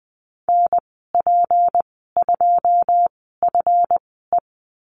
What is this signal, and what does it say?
Morse code, heard as a keyed tone, about 20 wpm, spaced as for 14 wpm, 700 Hz. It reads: NP2FE
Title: 167 Lesson 27g - Number 7 - Call Signs 20wpm